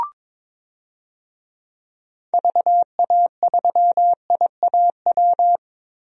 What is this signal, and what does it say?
Morse code, heard as a keyed tone, about 22 wpm, 700 Hz. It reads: VA3IAW
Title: DXCC-100 Call Signs - 6 Characters - Repeated 3x - 22wpm